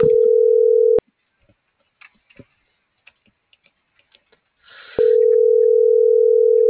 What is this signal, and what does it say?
Ringback tone (North America), 440 + 480 Hz.
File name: CENTRAL SKYBLUE / out axel.juarez